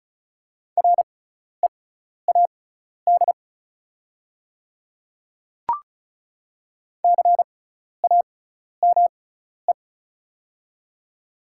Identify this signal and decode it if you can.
Morse code, heard as a keyed tone, about 36 wpm, spaced as for 12 wpm, 700 Hz. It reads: READ CAME
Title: 6x Character Spacing - Top 200 Words - 35wpm